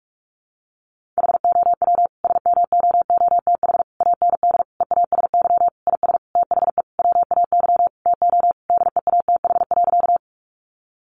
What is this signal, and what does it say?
Morse code, heard as a keyed tone, about 44 wpm, 700 Hz. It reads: HOW SMOOTH AND EASY IS THE WAY TO DEATH.
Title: Sentences from Top 1400 Words 45wpm